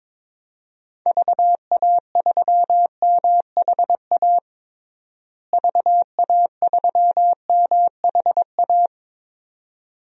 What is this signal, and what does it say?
Morse code, heard as a keyed tone, about 21 wpm, 700 Hz. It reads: VA3MHA VA3MHA
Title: DXCC-100 Call Signs - 6 Characters - Repeated 2x - 22wpm